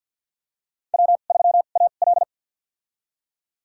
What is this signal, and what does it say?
Morse code, heard as a keyed tone, about 51 wpm, spaced as for 36 wpm, 700 Hz. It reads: W3AL
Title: USA Call Signs - 4 Characters - 2x Character Spacing - 50wpm